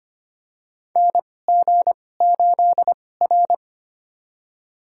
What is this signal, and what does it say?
Morse code, heard as a keyed tone, about 25 wpm, spaced as for 18 wpm, 700 Hz. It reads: NG8R
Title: USA Call Signs - 4 Characters - 2x Character Spacing - 25wpm